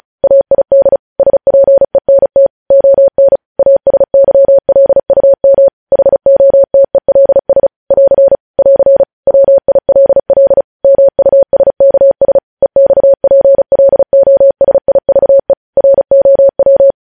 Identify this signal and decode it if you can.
Morse code, heard as a keyed tone, about 36 wpm, 550 Hz. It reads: AID SPENT ON ASYLUM HOTELS <AR> <AR> WILL MUSKS EXPLOSIVE ROW
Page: news headlines in morse code at 35 wpm